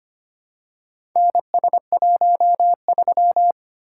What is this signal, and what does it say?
Morse code, heard as a keyed tone, about 25 wpm, 700 Hz. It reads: NS13
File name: practice / 25wpm / Four Letter-Number Combinations - 25wpm